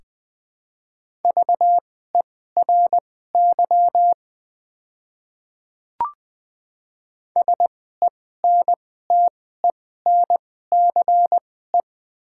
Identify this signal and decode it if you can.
Morse code, heard as a keyed tone, about 20 wpm, spaced as for 14 wpm, 700 Hz. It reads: VERY SENTENCE
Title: 2x Character Spacing - Top 200 Words - 20wpm